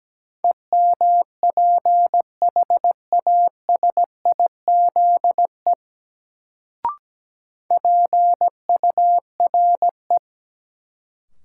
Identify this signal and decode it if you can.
Morse code, heard as a keyed tone, about 17 wpm, 700 Hz. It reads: EMPHASIZE PURE
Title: Sentences from Top 1500 Words - New Word Drill Alt - 17wpm